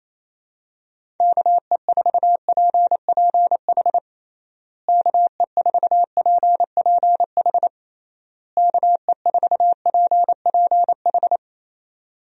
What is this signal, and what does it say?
Morse code, heard as a keyed tone, about 28 wpm, 700 Hz. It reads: KE4PPH KE4PPH KE4PPH
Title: USA Call Signs - 6 Characters - Repeated 3x - 28wpm